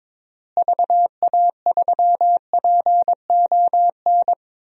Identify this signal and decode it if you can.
Morse code, heard as a keyed tone, about 22 wpm, 700 Hz. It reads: VA3PON
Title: DXCC-100 Call Signs - 6 Characters - Repeated 3x - 22wpm